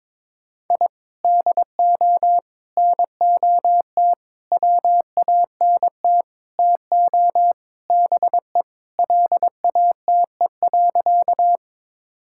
Morse code, heard as a keyed tone, about 22 wpm, 700 Hz. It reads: I DO NOT WANT TO BE LATE.
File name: practice / 22wpm / Sentences with 4-Word Phrases - 22wpm